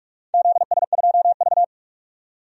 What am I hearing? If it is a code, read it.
Morse code, heard as a keyed tone, about 46 wpm, 700 Hz. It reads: ZS2V
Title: DXCC-100 Call Signs - 4 Characters - 45wpm